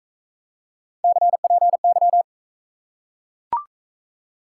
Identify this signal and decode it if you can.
Morse code, heard as a keyed tone, about 40 wpm, 700 Hz. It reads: CPY